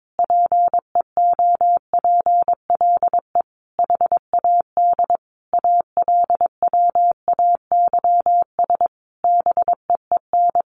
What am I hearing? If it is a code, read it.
Morse code, heard as a keyed tone, about 22 wpm, 700 Hz. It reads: PEOPLE HAD ALWAYS BEEN
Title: Sentences from Top 1400 Words 22wpm